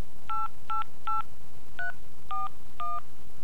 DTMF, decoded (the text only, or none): 888341